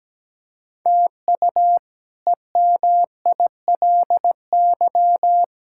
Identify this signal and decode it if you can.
Morse code, heard as a keyed tone, about 17 wpm, 700 Hz. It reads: TU EMILY
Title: K1USN SST QSO Practice - 17wpm